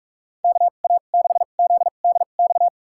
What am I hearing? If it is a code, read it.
Morse code, heard as a keyed tone, about 46 wpm, spaced as for 32 wpm, 700 Hz. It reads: KA6ZDX
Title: USA Call Signs - 6 Characters - 2x Character Spacing - 45wpm